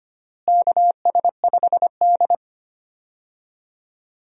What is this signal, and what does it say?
Morse code, heard as a keyed tone, about 25 wpm, 700 Hz. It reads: KS5D
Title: USA Call Signs - 4 Characters - Repeated 3x - 25wpm